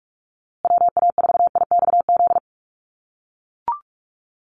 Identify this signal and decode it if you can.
Morse code, heard as a keyed tone, about 44 wpm, 700 Hz. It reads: WA4IXZ
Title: USA Call Signs Repeated 3x 45wpm